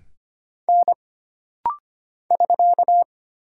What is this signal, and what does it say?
Morse code, heard as a keyed tone, about 25 wpm, 700 Hz. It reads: N <SK>
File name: Single Character (Letter:Number:Punctuation:Prosign) - Rapid-Fire - 25wpm